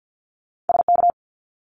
Morse code, heard as a keyed tone, about 50 wpm, 700 Hz. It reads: SK
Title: Canadian Provinces and Territories 50wpm